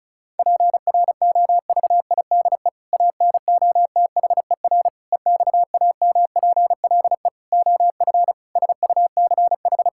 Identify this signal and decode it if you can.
Morse code, heard as a keyed tone, about 36 wpm, 700 Hz. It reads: PROVIDE ANOTHER EXAMPLE OF SUCH
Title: Sentences from Top 1000 Words - Review Part 5 - 35wpm